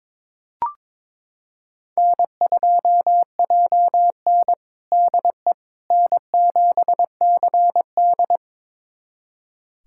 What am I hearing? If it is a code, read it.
Morse code, heard as a keyed tone, about 22 wpm, 700 Hz. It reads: N2JN DE N7CD
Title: USA Callsign DE Callsign - 4 Characters - 22wpm